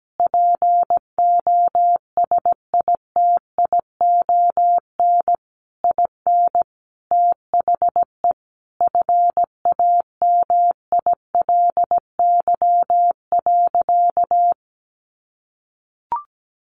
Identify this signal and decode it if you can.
Morse code, heard as a keyed tone, about 17 wpm, 700 Hz. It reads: POSITION IN THE FAMILY.
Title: Sentences from Top 1200 Words 17wpm